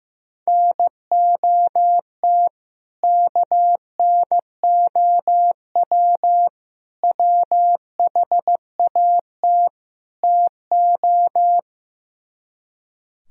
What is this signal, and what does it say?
Morse code, heard as a keyed tone, about 15 wpm, 700 Hz. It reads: NOT KNOW WHAT TO
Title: Speed-Racing - Sets of 4 Words 15wpm